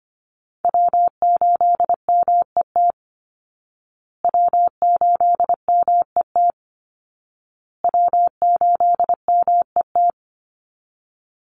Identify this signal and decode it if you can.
Morse code, heard as a keyed tone, about 25 wpm, 700 Hz. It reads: W8MET W8MET W8MET
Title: Call Signs Repeated 2x - 25wpm